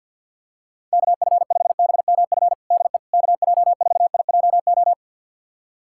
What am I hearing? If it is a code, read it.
Morse code, heard as a keyed tone, about 51 wpm, 700 Hz. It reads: KF5BMF DE KJ4IJO
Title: USA Callsign DE Callsign - 6 Characters - 50wpm